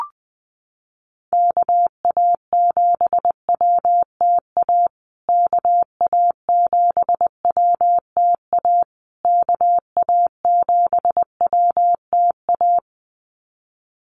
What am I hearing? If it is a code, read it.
Morse code, heard as a keyed tone, about 20 wpm, 700 Hz. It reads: KA7WTA KA7WTA KA7WTA
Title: USA Call Signs Repeated 3x 20wpm